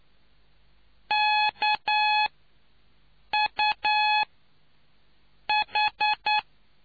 Morse code, heard as a keyed tone, about 9 wpm, 850 Hz. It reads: K U H